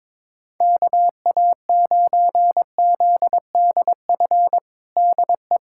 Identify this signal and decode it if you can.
Morse code, heard as a keyed tone, about 22 wpm, 700 Hz. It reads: KA9ZDF DE
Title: USA Callsign DE Callsign - 6 Characters - 22wpm